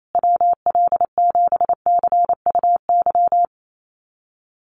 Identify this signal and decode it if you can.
Morse code, heard as a keyed tone, about 28 wpm, 700 Hz. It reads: WL7CUY
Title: USA Call Signs Repeated 2x 28wpm